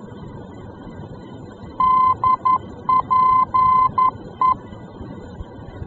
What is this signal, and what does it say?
Morse code, heard as a keyed tone, about 11 wpm, 1030 Hz. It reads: DPE
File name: france / mx / DPE-F-VOR